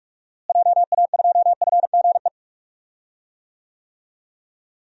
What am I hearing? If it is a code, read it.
Morse code, heard as a keyed tone, about 46 wpm, 700 Hz. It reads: JA2FGE
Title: DXCC-100 Call Signs - 6 Characters - Repeated 2x - 45wpm